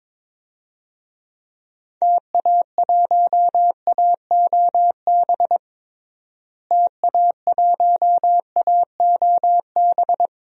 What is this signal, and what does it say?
Morse code, heard as a keyed tone, about 22 wpm, 700 Hz. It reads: TA1AOB TA1AOB
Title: DXCC-100 Call Signs - 6 Characters - Repeated 2x - 22wpm